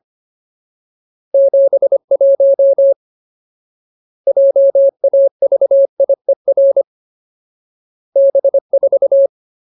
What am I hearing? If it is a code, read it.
Morse code, heard as a keyed tone, about 25 wpm, 550 Hz. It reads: 71 JAVIER B4
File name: Ultimate Word List / 25WPM Ultimate